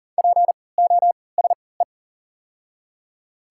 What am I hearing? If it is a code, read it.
Morse code, heard as a keyed tone, about 40 wpm, spaced as for 23 wpm, 700 Hz. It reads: POSE
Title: 3x Character Spacing - Top 500 Words - 40wpm